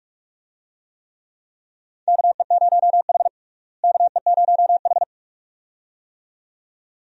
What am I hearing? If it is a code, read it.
Morse code, heard as a keyed tone, about 43 wpm, 700 Hz. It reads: KE0H KE0H
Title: USA Call Signs - 4 Characters - Repeated 2x - 45wpm